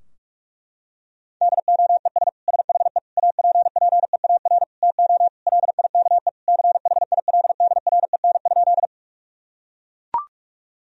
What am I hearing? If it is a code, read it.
Morse code, heard as a keyed tone, about 41 wpm, 700 Hz. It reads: DOES SHE APPEAR TO LIKE CHILDREN?
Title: Sentences from Top 1000 Words - Review Part 2 - 45wpm